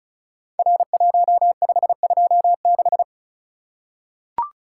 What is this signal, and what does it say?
Morse code, heard as a keyed tone, about 36 wpm, 700 Hz. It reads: R1526
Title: Code Groups - 5 Letter:Numbers - 35wpm